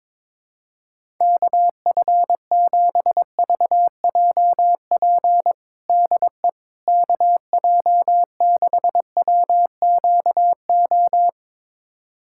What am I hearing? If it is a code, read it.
Morse code, heard as a keyed tone, about 21 wpm, 700 Hz. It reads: KF7VJP DE KJ6WQO
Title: USA Callsign DE Callsign 22wpm